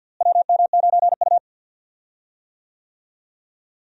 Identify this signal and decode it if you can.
Morse code, heard as a keyed tone, about 52 wpm, 700 Hz. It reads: WM9U
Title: USA Call Signs - 4 Characters - Repeated 2x - 50wpm